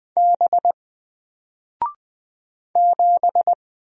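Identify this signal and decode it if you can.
Morse code, heard as a keyed tone, about 20 wpm, 700 Hz. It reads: B 7
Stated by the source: Single letter and number Morse drill at 20 wpm, from N4LCD.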